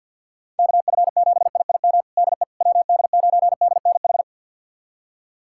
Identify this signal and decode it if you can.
Morse code, heard as a keyed tone, about 52 wpm, 700 Hz. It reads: KF7IIM DE WD9DNH
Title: USA Callsign DE Callsign - 6 Characters - 50wpm